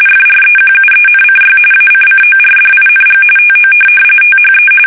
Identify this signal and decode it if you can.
RTTY, about 45.45 baud. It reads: FREQUENCY ACCORDING TO THE
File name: fsk850 / teletype-RTTY